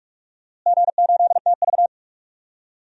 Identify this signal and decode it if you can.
Morse code, heard as a keyed tone, about 46 wpm, 700 Hz. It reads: G8TV
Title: DXCC-100 Call Signs - 4 Characters - Repeated 2x - 45wpm